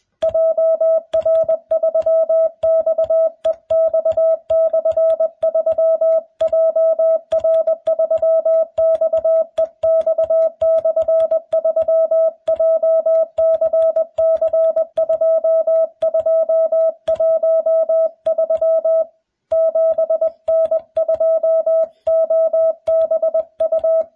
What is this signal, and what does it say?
Morse code, heard as a keyed tone, about 21 wpm, 650 Hz. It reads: JR3XEX/3 JR3XEX/3 JCC2213 7N2OBU